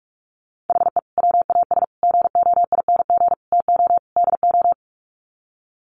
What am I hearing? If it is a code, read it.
Morse code, heard as a keyed tone, about 44 wpm, 700 Hz. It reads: HE WAS GOING TO DO
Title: Sets of 5 Words Encore2 45wpm